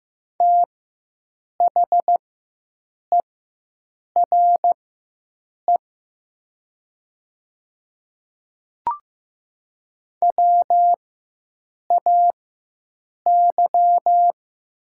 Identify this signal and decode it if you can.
Morse code, heard as a keyed tone, about 15 wpm, spaced as for 7 wpm, 700 Hz. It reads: THERE WAY
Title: 4x Character Spacing - Top 100 Words - 15wpm